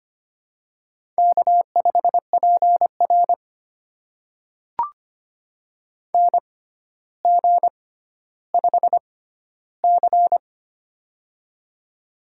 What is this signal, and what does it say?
Morse code, heard as a keyed tone, about 25 wpm, spaced as for 13 wpm, 700 Hz. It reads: K5PR NG5C
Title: USA Call Signs - 4 Characters - 6x Character Spacing - 25wpm